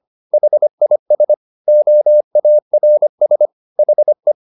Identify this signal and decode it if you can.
Morse code, heard as a keyed tone, about 25 wpm, 600 Hz. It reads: HIS OARS HE
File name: the old man and the sea part 1 25wpm